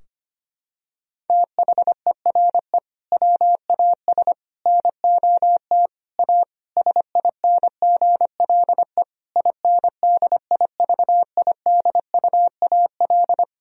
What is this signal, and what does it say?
Morse code, heard as a keyed tone, about 25 wpm, 690 Hz. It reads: THERE WAS NOT A SINGLE INDIVIDUAL
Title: Sentences from Top 1000 Words - Review Part 4 - 25wpm